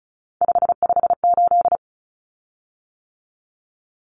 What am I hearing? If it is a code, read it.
Morse code, heard as a keyed tone, about 34 wpm, 700 Hz. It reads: 558
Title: Common Cut Numbers - 3-Digit 35wpm